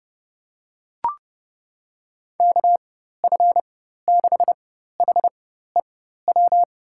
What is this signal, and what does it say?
Morse code, heard as a keyed tone, about 30 wpm, spaced as for 14 wpm, 700 Hz. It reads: KF6HEW
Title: USA Call Signs - 6 Characters - 4x Character Spacing - 30wpm